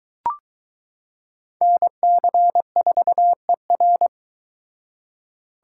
Morse code, heard as a keyed tone, about 22 wpm, 700 Hz. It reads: NC4ER